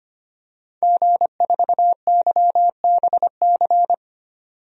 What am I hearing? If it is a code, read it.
Morse code, heard as a keyed tone, about 25 wpm, 700 Hz. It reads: G4YBC
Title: DXCC-100 Call Signs - 5 Characters - 25wpm